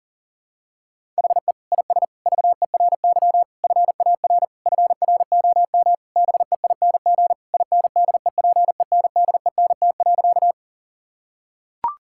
Morse code, heard as a keyed tone, about 37 wpm, 700 Hz. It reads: HE IS VERY FAR FROM BEING INDEPENDENT.
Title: Sentences from Top 1000 Words - Review Part 6 - 40wpm